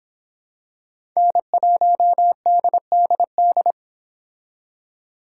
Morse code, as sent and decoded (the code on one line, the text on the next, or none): -. .---- -.. -.. -..
N1DDD